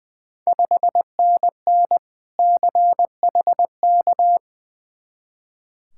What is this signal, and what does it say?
Morse code, heard as a keyed tone, about 20 wpm, 700 Hz. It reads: 5NN CHK
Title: Ontario QSO Party Exchange 20wpm